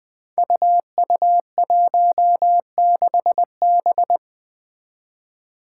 Morse code, heard as a keyed tone, about 20 wpm, 700 Hz. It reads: UU16B
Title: Five Challenging Letter-Number Combinations 20wpm